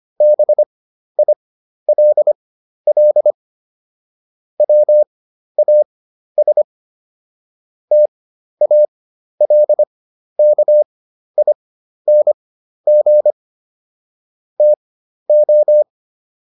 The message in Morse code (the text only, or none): BILL WAS TALKING TO